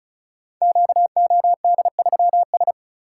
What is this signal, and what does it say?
Morse code, heard as a keyed tone, about 36 wpm, 700 Hz. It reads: QOD3S
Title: Code Groups - 5 Letter:Numbers - 35wpm